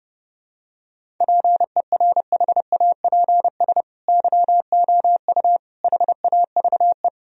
Morse code, heard as a keyed tone, about 30 wpm, 700 Hz. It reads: PERHAPS YOU HAVE
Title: Sentences from Top 1000 Words - Review Part 1 - 30wpm